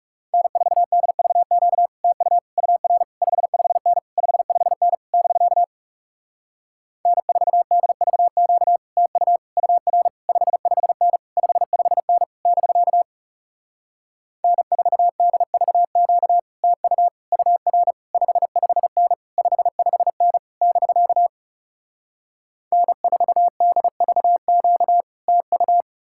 Morse code, heard as a keyed tone, about 40 wpm, 700 Hz. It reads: N4DVQ TU UR 55N 55N <BK> N4DVQ TU UR 55N 55N <BK> N4DVQ TU UR 55N 55N <BK> N4DVQ TU